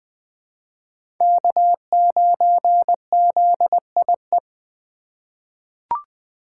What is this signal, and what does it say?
Morse code, heard as a keyed tone, about 21 wpm, 700 Hz. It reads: K9ZIE
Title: 2023 POTA Activator Call Signs - 20wpm